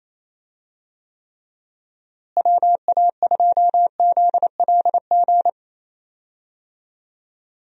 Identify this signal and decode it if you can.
Morse code, heard as a keyed tone, about 28 wpm, 700 Hz. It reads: WA2ZLG
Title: USA Call Signs - 6 Characters - Repeated 3x - 28wpm